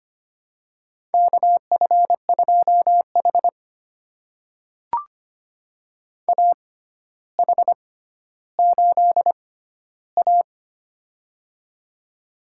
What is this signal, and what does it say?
Morse code, heard as a keyed tone, about 25 wpm, spaced as for 13 wpm, 700 Hz. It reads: KF2H AH8A
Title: USA Call Signs - 4 Characters - 6x Character Spacing - 25wpm